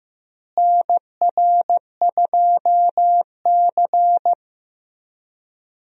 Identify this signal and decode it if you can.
Morse code, heard as a keyed tone, about 15 wpm, 700 Hz. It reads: NR2C
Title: USA Call Signs - 4 Characters - Repeated 3x - 15wpm